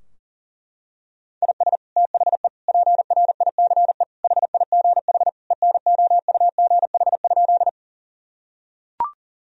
Morse code, heard as a keyed tone, about 40 wpm, 700 Hz. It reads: IS THE PRICE HIGH ENOUGH?
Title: Sentences from Top 1000 Words - Review Part 4 - 40wpm